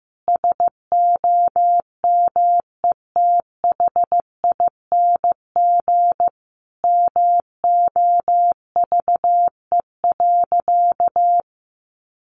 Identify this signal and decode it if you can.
Morse code, heard as a keyed tone, about 15 wpm, 700 Hz. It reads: SOMETHING MOVE.
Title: Sentences from Top 1200 Words 15wpm